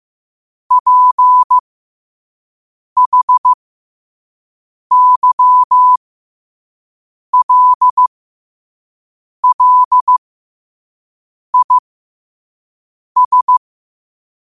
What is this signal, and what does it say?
Morse code, heard as a keyed tone, about 15 wpm, spaced as for 5 wpm, 1000 Hz. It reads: PHYLLIS
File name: qotdfiles / 5wpm